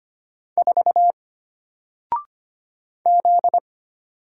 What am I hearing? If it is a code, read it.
Morse code, heard as a keyed tone, about 25 wpm, 700 Hz. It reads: 4 Z
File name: Single-Letter-Number-25wpm